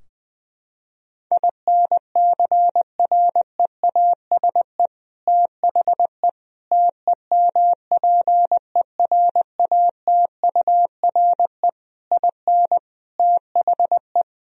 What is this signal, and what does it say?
Morse code, heard as a keyed tone, about 21 wpm, 700 Hz. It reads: INCREASE THE TEMPERATURE IN THE
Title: Sentences from Top 1000 Words - Review Part 4 - 20wpm